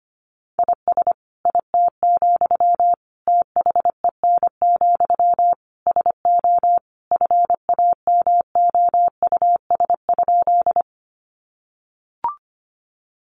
Morse code, heard as a keyed tone, about 25 wpm, 700 Hz. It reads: IS IT, THEN, SO FAMOUS?
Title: Sentences from Top 1500 Words 25wpm